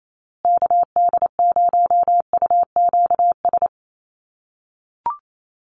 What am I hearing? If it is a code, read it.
Morse code, heard as a keyed tone, about 28 wpm, 700 Hz. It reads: KD0UQS
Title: USA Call Signs Repeated 3x 28wpm